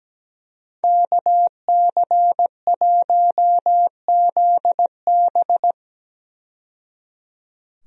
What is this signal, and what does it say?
Morse code, heard as a keyed tone, about 17 wpm, 700 Hz. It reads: KC1ZB
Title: USA Call Signs - 5 Characters - 17wpm